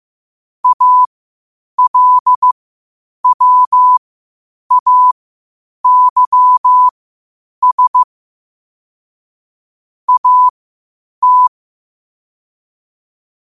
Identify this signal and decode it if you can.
Morse code, heard as a keyed tone, about 15 wpm, spaced as for 8 wpm, 1000 Hz. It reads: ALWAYS AT
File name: qotdfiles / 8wpm